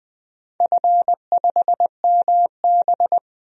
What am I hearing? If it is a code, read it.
Morse code, heard as a keyed tone, about 20 wpm, 700 Hz. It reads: F5MB
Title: DXCC-100 Call Signs - 4 Characters - 20wpm